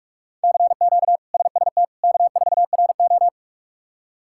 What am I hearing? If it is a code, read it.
Morse code, heard as a keyed tone, about 43 wpm, 700 Hz. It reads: CQ SST K4RO
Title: K1USN SST QSO Practice - 45wpm